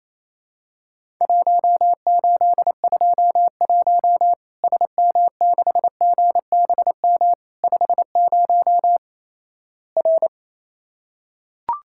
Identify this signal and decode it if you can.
Morse code, heard as a keyed tone, about 28 wpm, 700 Hz. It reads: 1821 SM6GBM 50 R
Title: Worked All Europe DX Contest - QTC Exchange - 28wpm